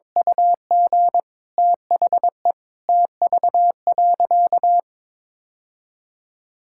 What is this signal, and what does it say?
Morse code, heard as a keyed tone, about 22 wpm, 690 Hz. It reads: UG THE TV.
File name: practice / 22wpm / Easy Sentences with Hints - 22wpm